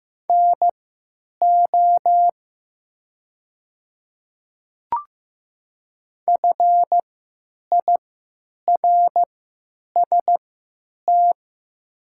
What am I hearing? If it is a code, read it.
Morse code, heard as a keyed tone, about 15 wpm, spaced as for 8 wpm, 700 Hz. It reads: NO FIRST
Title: 3x Character Spacing - Top 200 Words - 15wpm